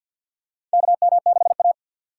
Morse code, heard as a keyed tone, about 52 wpm, 700 Hz. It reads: KM6A
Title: USA Call Signs - 2x Character Spacing - 50wpm